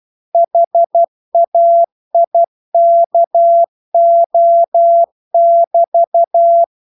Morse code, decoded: HAIKO<BT>